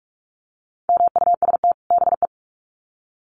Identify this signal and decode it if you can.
Morse code, heard as a keyed tone, about 44 wpm, 700 Hz. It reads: MUST BE
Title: Sets of 2 Words 45wpm